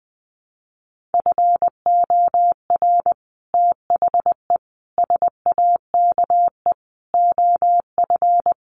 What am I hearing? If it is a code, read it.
Morse code, heard as a keyed tone, about 20 wpm, 700 Hz. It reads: FOR THE SAKE OF
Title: Sets of 5 Words Encore 20wpm